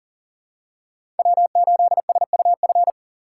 Morse code, heard as a keyed tone, about 40 wpm, 690 Hz. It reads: W8SUF
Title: USA Call Signs - 6x Character Spacing - 40wpm